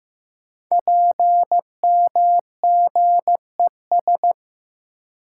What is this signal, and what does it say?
Morse code, heard as a keyed tone, about 15 wpm, 700 Hz. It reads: PMGES
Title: Code Groups - 5 Letters - 15wpm